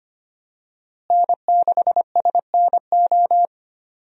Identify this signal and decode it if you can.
Morse code, heard as a keyed tone, about 25 wpm, 700 Hz. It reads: N6SNO
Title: USA Call Signs - 5 Characters - Repeated 2x - 25wpm